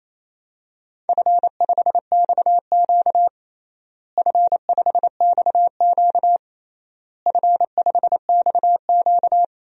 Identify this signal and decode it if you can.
Morse code, heard as a keyed tone, about 28 wpm, 700 Hz. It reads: F5XQ F5XQ F5XQ
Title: DXCC-100 Call Signs - 4 Characters - Repeated 3x - 28wpm